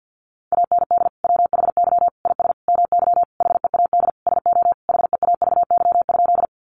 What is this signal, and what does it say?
Morse code, heard as a keyed tone, about 50 wpm, 700 Hz. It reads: AND WHY IS MY HEAD SO HEAVY?
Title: Sentences from Top 1100 Words 50wpm